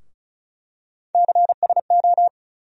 Morse code, heard as a keyed tone, about 36 wpm, 700 Hz. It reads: CSO
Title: USA Call Sign Postfixes - 3 Letters 35wpm